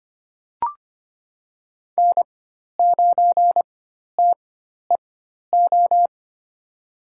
Morse code, heard as a keyed tone, about 25 wpm, spaced as for 12 wpm, 700 Hz. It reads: N9TEO
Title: USA Call Signs - 4x Character Spacing - 25wpm